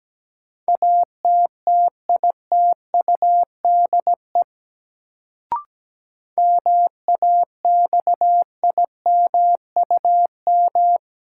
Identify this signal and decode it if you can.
Morse code, heard as a keyed tone, about 17 wpm, 700 Hz. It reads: ATTITUDE MAXIMUM
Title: Sentences from Top 1300 Words - New Word Drill Alt - 17wpm